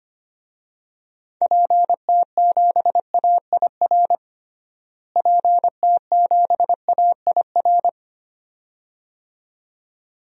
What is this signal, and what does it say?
Morse code, heard as a keyed tone, about 25 wpm, 700 Hz. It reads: PT7AIR PT7AIR
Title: DXCC-100 Call Signs - 6 Characters - Repeated 2x - 25wpm